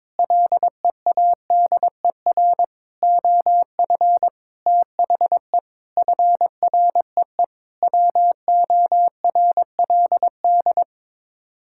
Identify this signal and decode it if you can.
Morse code, heard as a keyed tone, about 22 wpm, 700 Hz. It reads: LEADER OF THE FREE WORLD
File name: practice / 22wpm / Sets of 6 Words - 22wpm